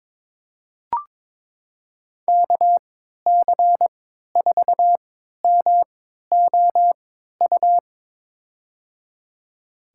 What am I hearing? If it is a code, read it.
Morse code, heard as a keyed tone, about 23 wpm, spaced as for 13 wpm, 700 Hz. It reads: KC4MOU
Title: USA Call Signs - 6 Characters - 3x Character Spacing - 22wpm